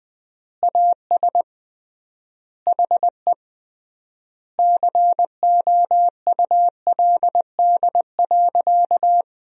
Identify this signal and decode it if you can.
Morse code, heard as a keyed tone, about 20 wpm, 700 Hz. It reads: AS HE COULD.
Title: 3x Word-Spacing - Sentences from Top 500 Words - 20wpm